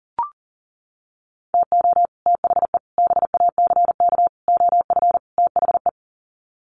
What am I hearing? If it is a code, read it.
Morse code, heard as a keyed tone, about 40 wpm, 700 Hz. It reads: TO THE BACK OF THE
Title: Sets of 5 Words 40wpm